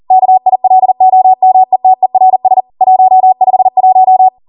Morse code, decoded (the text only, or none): KILOMETERS 151